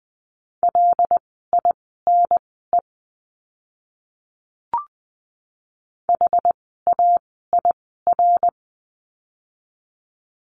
Morse code, heard as a keyed tone, about 20 wpm, spaced as for 14 wpm, 700 Hz. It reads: LINE HAIR
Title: 101 Lesson 17f - Letter u - Words 20wpm